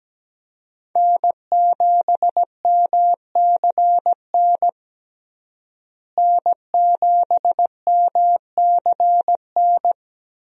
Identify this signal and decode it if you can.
Morse code, heard as a keyed tone, about 17 wpm, 700 Hz. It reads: N7MCN N7MCN
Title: USA Call Signs - 5 Characters - Repeated 2x - 17wpm